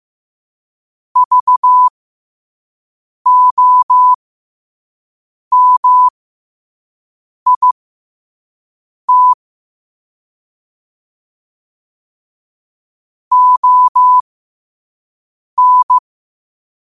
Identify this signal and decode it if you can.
Morse code, heard as a keyed tone, about 15 wpm, spaced as for 5 wpm, 1000 Hz. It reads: VOMIT ON
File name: qotdfiles / 5wpm